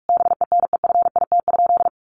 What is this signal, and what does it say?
Morse code, heard as a keyed tone, about 44 wpm, 700 Hz. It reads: BENEFIT?
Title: Sentences from Top 1500 Words 45wpm